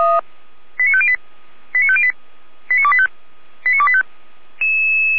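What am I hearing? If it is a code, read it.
Five-tone selective call (ZVEI), heard as quick runs of five tones, tones of about 70 ms. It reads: 89508 89508 89285 89285
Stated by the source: ZVEI five-tone selective call 89150, received on an SDR.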